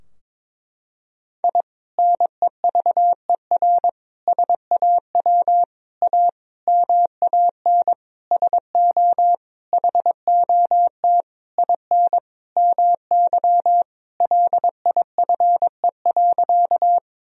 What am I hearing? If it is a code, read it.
Morse code, heard as a keyed tone, about 22 wpm, 700 Hz. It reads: I NEVER SAW A MAN SO HOT IN MY LIFE.
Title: Sentences from Top 1000 Words - Review Part 3 - 22wpm